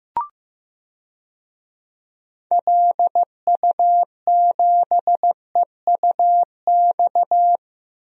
Morse code, heard as a keyed tone, about 15 wpm, 700 Hz. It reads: LU7EUX
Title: DXCC-100 Call Signs - 6 Characters - Repeated 3x - 15wpm